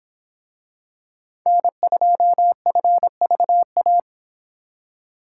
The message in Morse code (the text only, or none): N2FVA